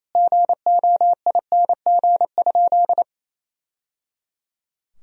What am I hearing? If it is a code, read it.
Morse code, heard as a keyed tone, about 28 wpm, 700 Hz. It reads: GOING?
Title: Sentences from Top 1000 Words - Review Part 2 - 28wpm